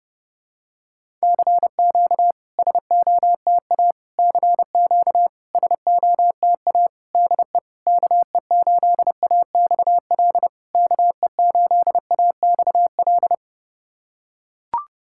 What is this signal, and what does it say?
Morse code, heard as a keyed tone, about 30 wpm, 690 Hz. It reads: CQ SOTA CQ SOTA DE KE8AXL KE8AXL
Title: SOTA QSO Practice 30wpm